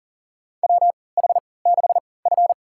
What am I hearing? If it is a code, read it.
Morse code, heard as a keyed tone, about 40 wpm, spaced as for 23 wpm, 700 Hz. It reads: WH6F
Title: USA Call Signs - 3x Character Spacing - 40wpm